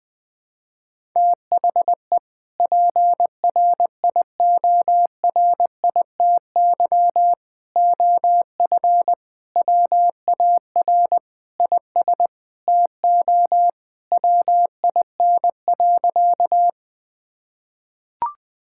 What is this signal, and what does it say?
Morse code, heard as a keyed tone, about 20 wpm, 700 Hz. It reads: THE PRIORITY OF WAR IS TO WIN.
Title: Sentences from Top 1600 Words 20wpm